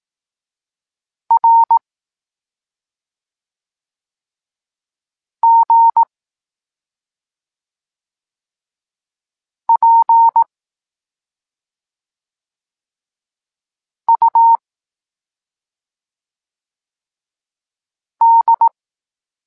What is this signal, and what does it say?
Morse code, heard as a keyed tone, about 18 wpm, 920 Hz. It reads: R G P U D